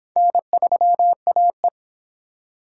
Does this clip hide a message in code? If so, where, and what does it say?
Yes: Morse code from 0.2 to 1.7 s, heard as a keyed tone, about 23 wpm, 700 Hz. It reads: N3AE